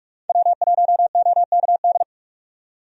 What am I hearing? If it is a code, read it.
Morse code, heard as a keyed tone, about 46 wpm, 700 Hz. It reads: W1OKD
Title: USA Call Signs - 5 Characters - 45wpm